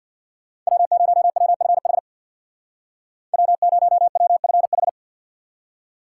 Morse code, heard as a keyed tone, about 62 wpm, 700 Hz. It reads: W0WFH W0WFH
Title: Speed Racing - USA Call Signs 50wpm